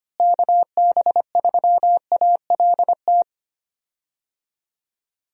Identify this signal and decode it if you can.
Morse code, heard as a keyed tone, about 25 wpm, 700 Hz. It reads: KB3ALT
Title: USA Call Signs - 6 Characters - Repeated 3x - 25wpm